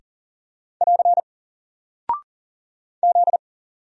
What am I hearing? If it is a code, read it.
Morse code, heard as a keyed tone, about 40 wpm, 700 Hz. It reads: <AR> Z
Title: Single Character (Letter:Number:Punctuation:Prosign) Alt 40wpm